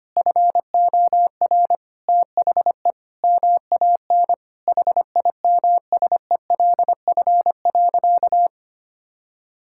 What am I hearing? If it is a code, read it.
Morse code, heard as a keyed tone, about 25 wpm, 700 Hz. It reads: FOR THE MAN HIMSELF.
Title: Sentences from Top 1600 Words 25wpm